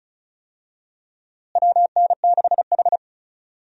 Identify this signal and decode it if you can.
Morse code, heard as a keyed tone, about 34 wpm, 700 Hz. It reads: WN6H
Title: USA Call Signs - 4 Characters - Repeated 2x - 35wpm